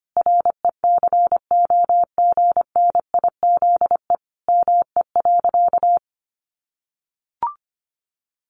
Morse code, heard as a keyed tone, about 25 wpm, 700 Hz. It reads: RECOGNIZE ME.
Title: Sentences from Top 1400 Words 25wpm